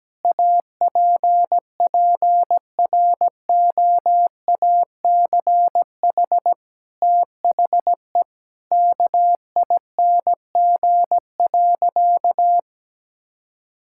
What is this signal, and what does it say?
Morse code, heard as a keyed tone, about 17 wpm, 700 Hz. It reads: APPROACH THE KING.
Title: Sentences from Top 1000 Words - Review Part 6 - 17wpm